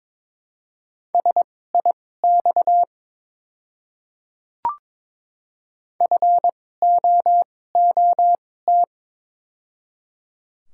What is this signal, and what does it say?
Morse code, heard as a keyed tone, about 22 wpm, spaced as for 16 wpm, 690 Hz. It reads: SIX FOOT
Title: 2x Character Spacing - Top 500 Words - 22wpm